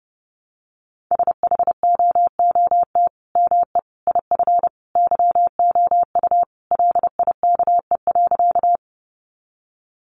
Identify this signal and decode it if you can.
Morse code, heard as a keyed tone, about 30 wpm, 700 Hz. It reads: SHOOT ME IF YOU LIKE.
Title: Sentences from Top 1200 Words 30wpm